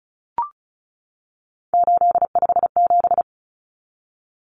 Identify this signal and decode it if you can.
Morse code, heard as a keyed tone, about 34 wpm, 700 Hz. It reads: 857
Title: Numbers - 3-Digit 35wpm